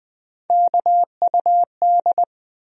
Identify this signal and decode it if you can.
Morse code, heard as a keyed tone, about 20 wpm, 700 Hz. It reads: KUD
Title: USA Call Sign Postfixes - 3 Letters 20wpm